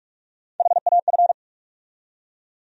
Morse code, heard as a keyed tone, about 46 wpm, 700 Hz. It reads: HAF
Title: USA Call Sign Postfixes - 3 Letters 45wpm